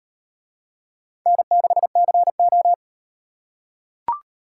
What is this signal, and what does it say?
Morse code, heard as a keyed tone, about 38 wpm, 700 Hz. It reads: N6CO